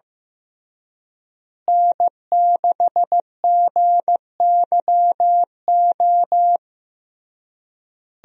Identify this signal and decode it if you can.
Morse code, heard as a keyed tone, about 15 wpm, 700 Hz. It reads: N6GYO
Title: USA Call Signs - 5 Characters - Repeated 3x - 15wpm